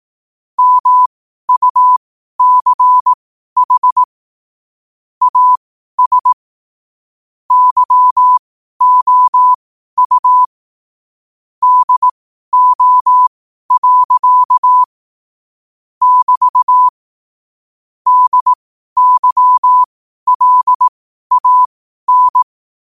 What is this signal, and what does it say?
Morse code, heard as a keyed tone, about 18 wpm, spaced as for 12 wpm, 1000 Hz. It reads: MUCH AS YOU DO. <BT> DYLAN
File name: qotdfiles / 12wpm